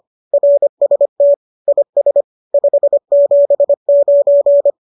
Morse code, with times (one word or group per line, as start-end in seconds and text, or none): 0.3-1.3 RST
1.7-2.2 IS
2.5-4.7 579